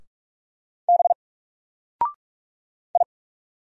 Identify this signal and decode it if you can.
Morse code, heard as a keyed tone, about 44 wpm, 700 Hz. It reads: B I
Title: Single Character (Letter:Number:Punctuation:Prosign) Alt 45wpm